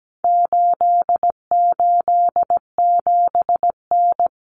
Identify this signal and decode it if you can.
Morse code, heard as a keyed tone, about 17 wpm, 700 Hz. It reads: 887N
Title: Common Cut Numbers - 4-Digit 17wpm